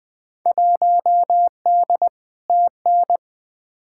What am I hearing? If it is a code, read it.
Morse code, heard as a keyed tone, about 21 wpm, 700 Hz. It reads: 1D TN
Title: Field Day Exchanges - 20wpm